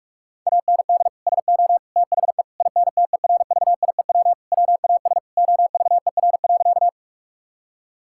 Morse code, heard as a keyed tone, about 46 wpm, 700 Hz. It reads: AND SO THE INTERVIEW WAS OVER.
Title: Sentences from Top 1600 Words 45wpm